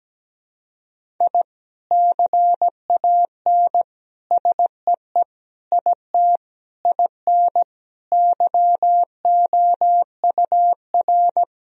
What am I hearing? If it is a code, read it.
Morse code, heard as a keyed tone, about 17 wpm, 700 Hz. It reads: I CAN SEE IT IN YOUR
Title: Sentences from Top 1000 Words - Review Part 1 - 17wpm